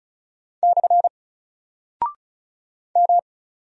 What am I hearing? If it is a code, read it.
Morse code, heard as a keyed tone, about 36 wpm, 700 Hz. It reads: / M
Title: Single Character (Letter:Number:Punctuation:Prosign) 35wpm